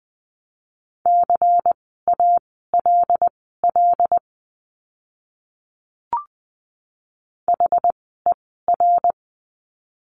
Morse code, heard as a keyed tone, about 20 wpm, spaced as for 15 wpm, 700 Hz. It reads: CALL HER
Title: 108 Lesson 18f - Letter m - Words 20wpm